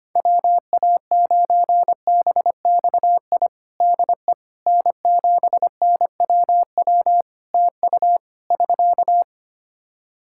Morse code, heard as a keyed tone, about 25 wpm, 700 Hz. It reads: WA9BXI DE N7NWW TU <SK>
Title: Practice QSO - Exchange 4 - Ending - 25wpm